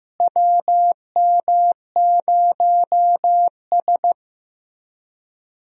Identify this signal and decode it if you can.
Morse code, heard as a keyed tone, about 15 wpm, 700 Hz. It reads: WM0S
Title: USA Call Signs - 4 Characters - Repeated 3x - 15wpm